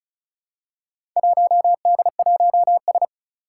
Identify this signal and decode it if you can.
Morse code, heard as a keyed tone, about 36 wpm, 700 Hz. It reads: 1D1S
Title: Four Challenging Letter-Number Combinations 35wpm